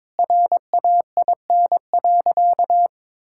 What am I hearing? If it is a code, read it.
Morse code, heard as a keyed tone, about 22 wpm, 690 Hz. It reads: RAIN.
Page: morse-code-ninja-practice-22wpm